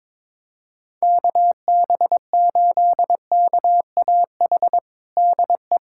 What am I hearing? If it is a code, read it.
Morse code, heard as a keyed tone, about 22 wpm, 700 Hz. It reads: KB8KAH DE
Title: USA Callsign DE Callsign - 6 Characters - 22wpm